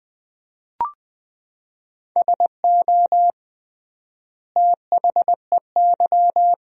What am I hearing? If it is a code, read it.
Morse code, heard as a keyed tone, about 20 wpm, 700 Hz. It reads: SO THEY
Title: 3x Word-Spacing - Sentences from Top 300 Words - 20wpm